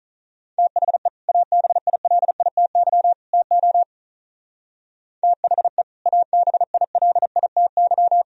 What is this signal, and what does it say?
Morse code, heard as a keyed tone, about 44 wpm, 700 Hz. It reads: THE ABILITY TO THE ABILITY